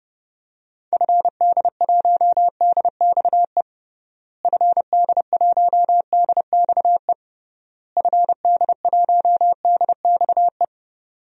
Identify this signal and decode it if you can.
Morse code, heard as a keyed tone, about 32 wpm, 700 Hz. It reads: FD1DXE FD1DXE FD1DXE
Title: DXCC-100 Call Signs - 6 Characters - Repeated 3x - 30wpm